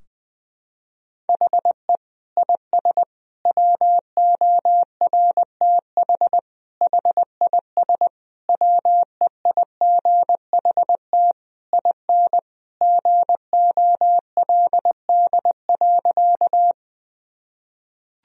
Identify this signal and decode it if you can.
Morse code, heard as a keyed tone, about 20 wpm, 700 Hz. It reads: HE IS WORTH HIS WEIGHT IN GOLD.
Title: Sentences from Top 1000 Words - Review Part 3 - 20wpm